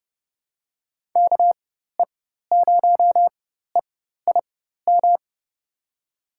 Morse code, heard as a keyed tone, about 30 wpm, spaced as for 14 wpm, 700 Hz. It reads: KE0EIM
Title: USA Call Signs - 4x Character Spacing - 30wpm